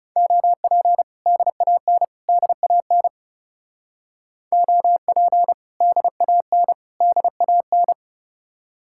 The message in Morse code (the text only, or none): OP DAN DAN OP DAN DAN